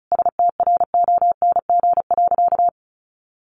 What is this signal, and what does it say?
Morse code, heard as a keyed tone, about 34 wpm, 700 Hz. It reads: STRONG.
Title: Sentences from Top 600 Words 35wpm